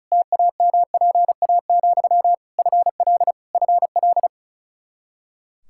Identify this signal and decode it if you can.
Morse code, heard as a keyed tone, about 36 wpm, 700 Hz. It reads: TAMPA, FL FL
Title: K1USN SST QSO Practice 35wpm